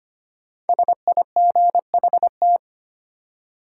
Morse code, heard as a keyed tone, about 25 wpm, 700 Hz. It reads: SIGHT
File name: 5 Letter Words - First 3 Letters Given - 25wpm